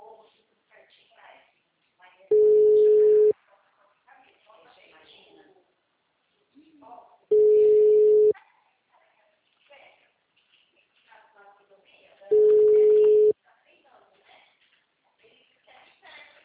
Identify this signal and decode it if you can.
Ringback tone (Europe, 425 Hz), with speech.